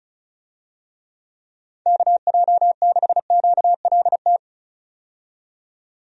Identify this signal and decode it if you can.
Morse code, heard as a keyed tone, about 36 wpm, 700 Hz. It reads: KJ6QLT
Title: USA Call Signs - 6 Characters - 35wpm